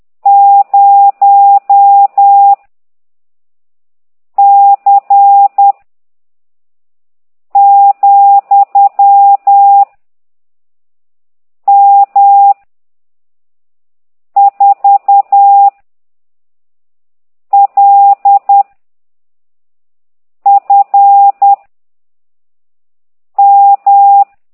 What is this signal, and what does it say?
Morse code, heard as a keyed tone, about 10 wpm, 800 Hz. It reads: 0 C , M 4 L F M
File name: ct2idw / exerc15